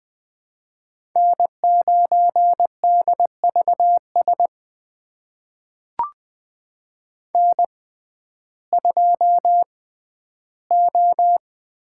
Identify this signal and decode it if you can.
Morse code, heard as a keyed tone, about 20 wpm, spaced as for 12 wpm, 700 Hz. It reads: N9DVS N2O
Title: USA Call Signs - 5 Characters - 6x Character Spacing - 20wpm